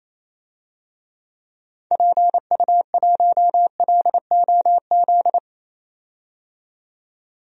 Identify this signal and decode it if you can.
Morse code, heard as a keyed tone, about 27 wpm, 700 Hz. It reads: PU1LOZ
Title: DXCC-100 Call Signs - 6 Characters - 28wpm